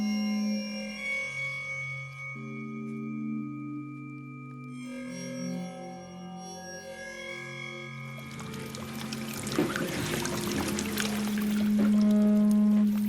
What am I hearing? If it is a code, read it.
Dial-up modem handshake.